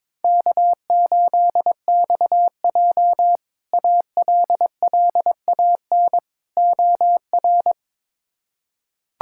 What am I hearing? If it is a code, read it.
Morse code, heard as a keyed tone, about 22 wpm, 700 Hz. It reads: K8XJ ALLAN OR
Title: K1USN SST QSO Practice - 22wpm